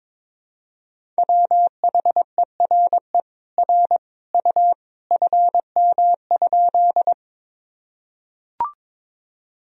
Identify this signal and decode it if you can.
Morse code, heard as a keyed tone, about 22 wpm, 700 Hz. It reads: WHERE R U FM?
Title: Sentences from Top 200 Words and Abbreviations 22wpm